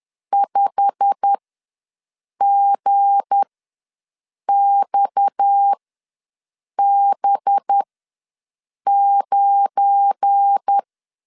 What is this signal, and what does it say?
Morse code, heard as a keyed tone, about 10 wpm, spaced as for 6 wpm, 800 Hz. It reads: 5GXB9
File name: LEZIONI / 8 YJ45 / 8 YJ45 ws 10wpm 3843